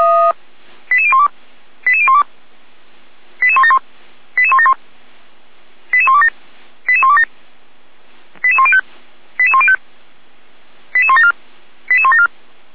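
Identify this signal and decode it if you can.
Five-tone selective call (ZVEI), heard as quick runs of five tones, tones of about 70 ms. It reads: 88012 88012 88171 88171 88127 88127 88105 88105 88174 88174
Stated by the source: ZVEI five-tone call received by SDR, logged on the audio server of Feuerwehr Orlamünde (fire brigade).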